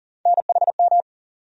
Morse code, heard as a keyed tone, about 40 wpm, 700 Hz. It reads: NHM
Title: USA Call Sign Postfixes - 3 Letters 40wpm